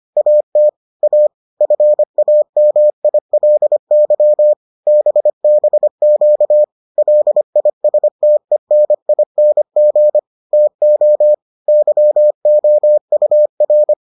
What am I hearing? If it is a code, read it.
Morse code, heard as a keyed tone, about 25 wpm, 600 Hz. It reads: AT A FAMILY BBQ LISTENING TO YOUR